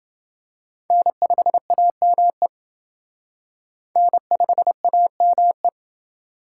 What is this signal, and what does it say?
Morse code, heard as a keyed tone, about 29 wpm, 700 Hz. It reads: N5AME N5AME